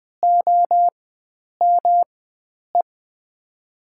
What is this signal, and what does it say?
Morse code, heard as a keyed tone, about 20 wpm, spaced as for 9 wpm, 700 Hz. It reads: OME
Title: USA Call Signs - 4x Character Spacing - 20wpm